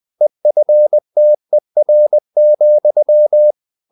Morse code, heard as a keyed tone, about 20 wpm, 600 Hz. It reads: EFTER,
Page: Ferie 20 wpm